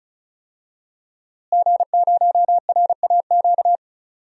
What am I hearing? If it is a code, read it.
Morse code, heard as a keyed tone, about 36 wpm, 700 Hz. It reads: G0RAQ
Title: DXCC-100 Call Signs - 5 Characters - Repeated 2x - 35wpm